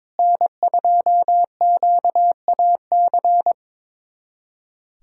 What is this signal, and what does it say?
Morse code, heard as a keyed tone, about 22 wpm, 700 Hz. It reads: N2QAC
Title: USA Callsign DE Callsign - 5 Characters - 22wpm